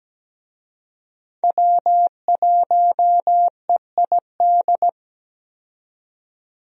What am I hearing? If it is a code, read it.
Morse code, heard as a keyed tone, about 17 wpm, 700 Hz. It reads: W1EID
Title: USA Call Signs - 5 Characters - 17wpm